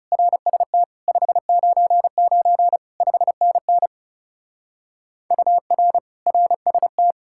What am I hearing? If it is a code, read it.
Morse code, heard as a keyed tone, about 36 wpm, 700 Hz. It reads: RST 599 5NN UR RST
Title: Speed-Racing Common QSO Phrases - 30wpm